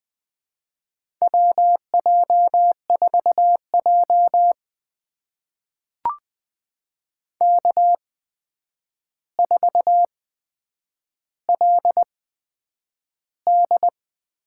Morse code, heard as a keyed tone, about 20 wpm, spaced as for 9 wpm, 700 Hz. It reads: WJ4J K4LD
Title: USA Call Signs - 4 Characters - 8x Character Spacing - 20wpm